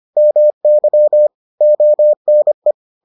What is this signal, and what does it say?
Morse code, heard as a keyed tone, about 25 wpm, 600 Hz. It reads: MY ONE